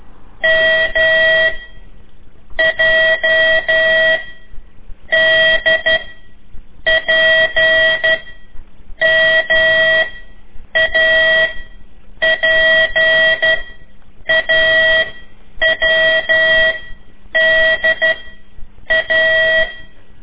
Morse code, heard as a keyed tone, about 12 wpm, 1950 Hz, with noise. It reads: M J D P M A P A W D A
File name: ik1weg2 / CORSI CW / lez4-3